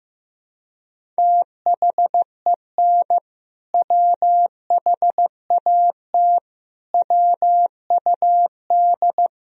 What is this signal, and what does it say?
Morse code, heard as a keyed tone, about 16 wpm, 700 Hz. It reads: THEN WHAT WUD
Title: Sentences from Top 400 Words and Abbreviations 15wpm